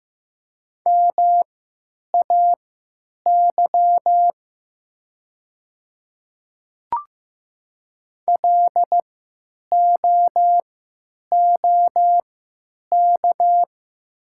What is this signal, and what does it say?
Morse code, heard as a keyed tone, about 15 wpm, spaced as for 9 wpm, 690 Hz. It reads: MAY LOOK
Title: 3x Character Spacing - Top 100 Words - 15wpm